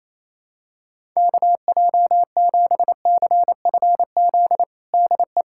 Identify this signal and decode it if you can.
Morse code, heard as a keyed tone, about 27 wpm, 700 Hz. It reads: KJ7CFZ DE
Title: USA Callsign DE Callsign - 6 Characters - 28wpm